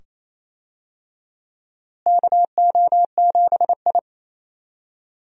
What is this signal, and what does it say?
Morse code, heard as a keyed tone, about 28 wpm, 700 Hz. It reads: KO7I
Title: USA Call Signs - 4 Characters - Repeated 2x - 28wpm